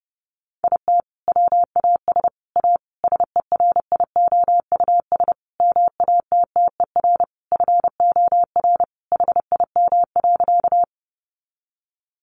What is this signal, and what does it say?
Morse code, heard as a keyed tone, about 30 wpm, 700 Hz. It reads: IT WAS A SERIOUS MATTER FOR HIM.
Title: Sentences from Top 1100 Words 30wpm